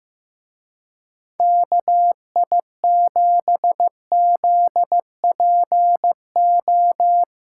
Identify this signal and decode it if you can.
Morse code, heard as a keyed tone, about 16 wpm, 700 Hz. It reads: KI7ZPO
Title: Speed Racing - USA Call Signs 15wpm